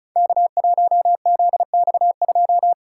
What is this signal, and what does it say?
Morse code, heard as a keyed tone, about 36 wpm, 700 Hz. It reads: K1ZX2
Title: Code Groups - 5 Letter:Numbers - 35wpm